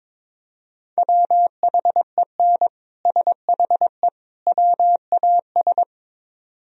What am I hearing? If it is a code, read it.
Morse code, heard as a keyed tone, about 24 wpm, 700 Hz. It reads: WHEN SHE WAS
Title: Speed-Racing - Sets of 3 Words 22wpm